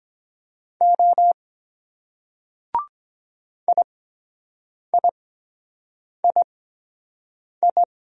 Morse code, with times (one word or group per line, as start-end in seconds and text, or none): 0.8-1.3 O
3.7-7.8 IIII